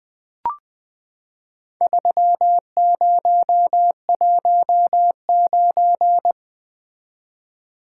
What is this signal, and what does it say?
Morse code, heard as a keyed tone, about 20 wpm, 700 Hz. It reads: 3019
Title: Four Letter-Number Combinations - 20wpm